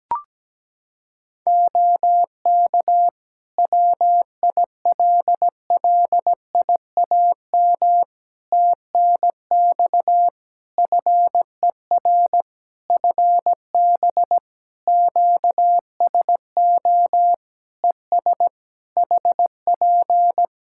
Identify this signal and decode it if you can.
Morse code, heard as a keyed tone, about 17 wpm, 700 Hz. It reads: OK WILLIAM TNX FER FB QSO ES HP
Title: Practice QSO - Exchange 4 - Ending - 17wpm